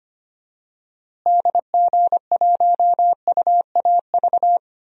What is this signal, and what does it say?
Morse code, heard as a keyed tone, about 25 wpm, 700 Hz. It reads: DG1UAV
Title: DXCC-100 Call Signs - 6 Characters - Repeated 2x - 25wpm